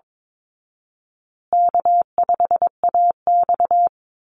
Morse code, heard as a keyed tone, about 22 wpm, 700 Hz. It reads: K5AX
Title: Easy Call Signs Repeated 3x - 22wpm